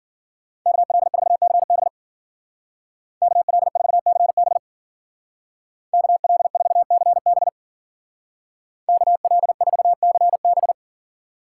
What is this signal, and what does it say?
Morse code, heard as a keyed tone, about 50 wpm, 700 Hz. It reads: KL4CB KL4CB KL4CB KL4CB
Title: Speed Racing - USA Call Signs 40wpm